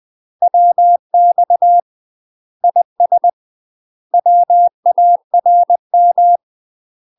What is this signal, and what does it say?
Morse code, heard as a keyed tone, about 20 wpm, 700 Hz. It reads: WX IS WARM